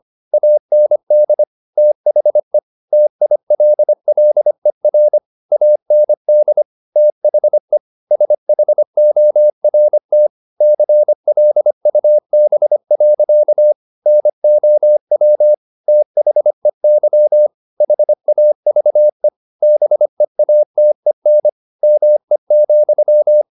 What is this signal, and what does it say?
Morse code, heard as a keyed tone, about 25 wpm, 600 Hz. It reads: AND THE TILLER AND THE SHORT CLUB. NOW THEY HAVE BEATEN ME,